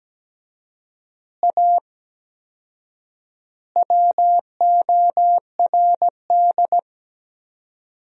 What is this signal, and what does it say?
Morse code, heard as a keyed tone, about 17 wpm, 700 Hz. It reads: A WORD
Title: 4x Word-Spacing - Sentences from Top 300 Words - 17wpm